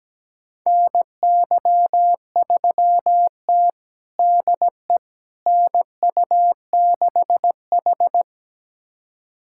Morse code, heard as a keyed tone, about 17 wpm, 700 Hz. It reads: NY3T DE NU6H
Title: USA Callsign DE Callsign - 4 Characters - 17wpm